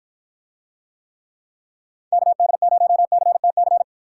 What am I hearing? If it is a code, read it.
Morse code, heard as a keyed tone, about 53 wpm, 700 Hz. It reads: KD0CTC